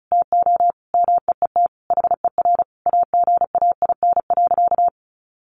Morse code, heard as a keyed tone, about 34 wpm, 700 Hz. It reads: TO MEET HER AGAIN.
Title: Sentences from Top 1100 Words 35wpm